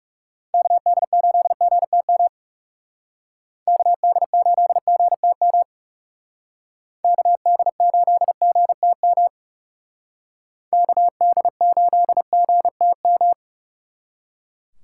Morse code, heard as a keyed tone, about 36 wpm, 700 Hz. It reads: KD8GTM KD8GTM KD8GTM KD8GTM
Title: Speed Racing - USA 2x3 Call Signs 30wpm